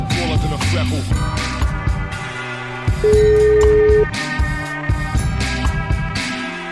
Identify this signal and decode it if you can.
Ringback tone (Europe, 425 Hz), with noise.